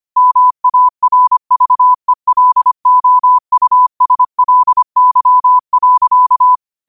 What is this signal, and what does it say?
Morse code, heard as a keyed tone, about 25 wpm, 1000 Hz. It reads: MARVELOUSLY.
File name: qotdfiles / 25wpm